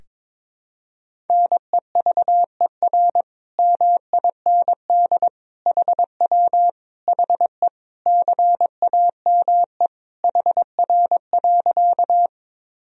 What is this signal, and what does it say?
Morse code, heard as a keyed tone, about 22 wpm, 700 Hz. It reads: NEVER MIND HW HE CAME HR.
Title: Sentences from Top 500 Words and Abbreviations 22wpm